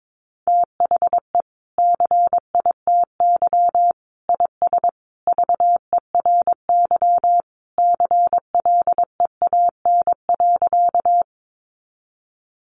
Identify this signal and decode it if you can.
Morse code, heard as a keyed tone, about 22 wpm, 700 Hz. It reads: THE CITY IS VERY CLEAN.
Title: Easy Phrases 22wpm